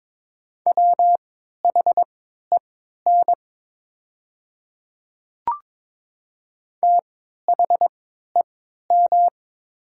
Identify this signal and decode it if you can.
Morse code, heard as a keyed tone, about 22 wpm, spaced as for 12 wpm, 700 Hz. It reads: WHEN THEM
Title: 3x Character Spacing - Top 100 Words - 22wpm